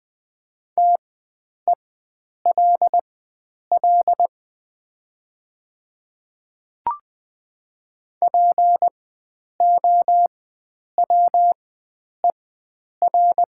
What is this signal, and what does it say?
Morse code, heard as a keyed tone, about 20 wpm, spaced as for 9 wpm, 700 Hz. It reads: TELL POWER
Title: 4x Character Spacing - Top 500 Words - 20wpm